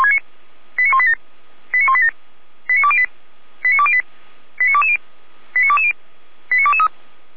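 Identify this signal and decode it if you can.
Five-tone selective call (ZVEI), heard as quick runs of five tones, tones of about 70 ms. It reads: 89187 89187 89208 89208 89220 89220 89223
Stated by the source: ZVEI five-tone selective call received by SDR, logged by feuerwehr-orlamuende (fire brigade Orlamünde).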